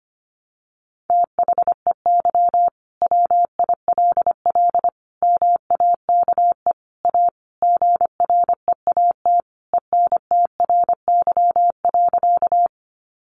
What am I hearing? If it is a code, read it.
Morse code, heard as a keyed tone, about 25 wpm, 700 Hz. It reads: THEY WILL MAKE A GREAT ENTRY.
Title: Sentences from Top 1500 Words 25wpm